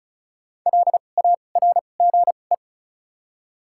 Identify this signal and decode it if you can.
Morse code, heard as a keyed tone, about 34 wpm, spaced as for 25 wpm, 700 Hz. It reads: LARGE
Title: 2x Character Spacing - Top 200 Words - 35wpm